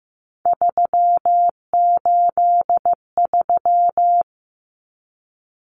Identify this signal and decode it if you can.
Morse code, heard as a keyed tone, about 15 wpm, 700 Hz. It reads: 383
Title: Numbers - 3-Digit 15wpm